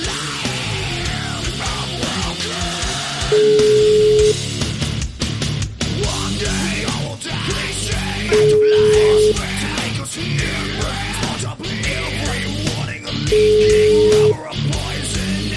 Ringback tone (Europe, 425 Hz), with noise.